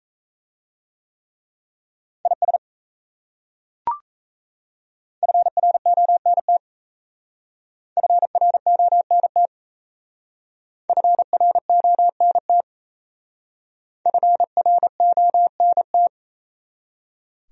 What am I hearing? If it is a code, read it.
Morse code, heard as a keyed tone, about 33 wpm, 700 Hz. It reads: IS FRONT FRONT FRONT FRONT
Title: Speed-Racing Top 500 Words 28wpm